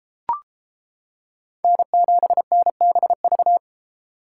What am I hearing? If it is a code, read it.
Morse code, heard as a keyed tone, about 33 wpm, 700 Hz. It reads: N7NBV